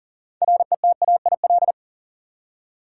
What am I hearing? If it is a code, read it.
Morse code, heard as a keyed tone, about 40 wpm, 700 Hz. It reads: RETAIL
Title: Practice QSO - Exchange 3 - More Info - 40wpm